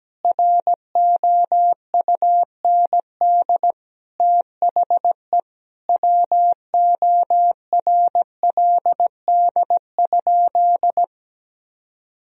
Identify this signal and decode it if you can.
Morse code, heard as a keyed tone, about 17 wpm, 700 Hz. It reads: ROUND THE WORLD?
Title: Sentences from Top 1000 Words - Review Part 1 - 17wpm